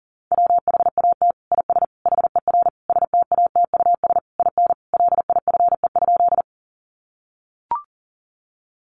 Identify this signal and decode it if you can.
Morse code, heard as a keyed tone, about 40 wpm, 700 Hz. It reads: WHAT IS HER STATUS IN LIFE?
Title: Sentences from Top 1400 Words 40wpm